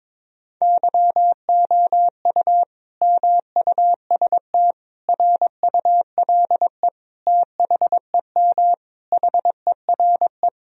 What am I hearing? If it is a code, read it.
Morse code, heard as a keyed tone, about 22 wpm, 700 Hz. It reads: YOU MUST RULE THEM HERE
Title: Sentences from Top 1000 Words - Review Part 2 - 22wpm